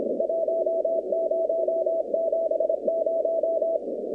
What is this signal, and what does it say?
Morse code, heard as a keyed tone, about 30 wpm, 650 Hz, with noise. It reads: 1070
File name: contests / 2019-cw-wpx-kt5j-k5pi- -2330 0-47